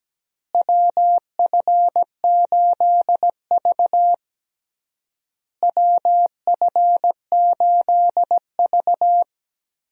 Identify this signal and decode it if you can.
Morse code, heard as a keyed tone, about 17 wpm, 700 Hz. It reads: WF8V WF8V
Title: USA Call Signs - 4 Characters - Repeated 3x - 17wpm